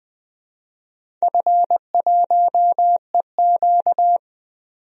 Morse code, heard as a keyed tone, about 20 wpm, 700 Hz. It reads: F1EQ
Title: DXCC-100 Call Signs - 4 Characters - 20wpm